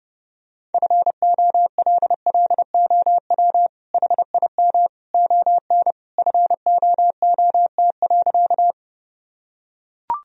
Morse code, heard as a keyed tone, about 30 wpm, 700 Hz. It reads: FOLLOW HIM ON FOOT.
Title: Sentences from Top 1000 Words - Review Part 3 - 30wpm